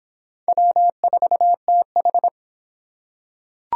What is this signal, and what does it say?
Morse code, heard as a keyed tone, about 24 wpm, 700 Hz. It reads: W4TH